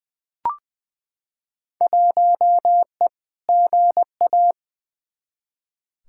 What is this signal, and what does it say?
Morse code, heard as a keyed tone, about 20 wpm, 700 Hz. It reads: 1E GA
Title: Field Day Exchanges - 20wpm